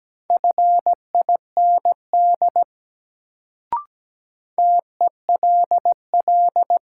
Morse code, heard as a keyed tone, about 17 wpm, 700 Hz. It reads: FIND TELL
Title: Sets of 2 Words - New Word Drill - 17wpm